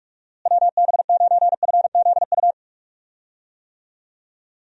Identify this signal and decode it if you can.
Morse code, heard as a keyed tone, about 46 wpm, 700 Hz. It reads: WB9FZU
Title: USA Call Signs - 6 Characters - Repeated 2x - 45wpm